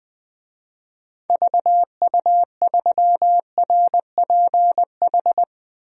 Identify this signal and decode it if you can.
Morse code, heard as a keyed tone, about 21 wpm, 700 Hz. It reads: VU3RPH
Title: DXCC-100 Call Signs - 6 Characters - 20wpm